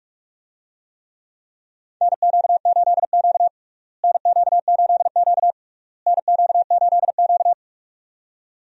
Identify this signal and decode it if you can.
Morse code, heard as a keyed tone, about 44 wpm, 700 Hz. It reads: NQ8Q NQ8Q NQ8Q
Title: USA Call Signs - 4 Characters - Repeated 3x - 45wpm